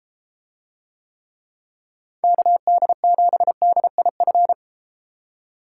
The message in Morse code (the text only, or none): KD7DIF